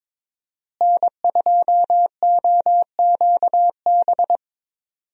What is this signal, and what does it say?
Morse code, heard as a keyed tone, about 22 wpm, 700 Hz. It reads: N2OQB
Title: USA Call Signs - 5 Characters - Repeated 3x - 22wpm